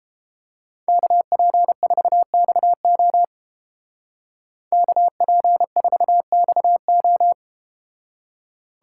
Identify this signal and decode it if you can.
Morse code, heard as a keyed tone, about 31 wpm, 700 Hz. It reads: KP4XO KP4XO